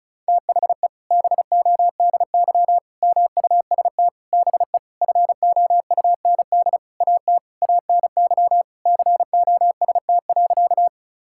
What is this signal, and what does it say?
Morse code, heard as a keyed tone, about 36 wpm, 700 Hz. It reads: THE BODY MUST BE FOUND AT ANY COST.
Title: Sentences from Top 1000 Words - Review Part 5 - 35wpm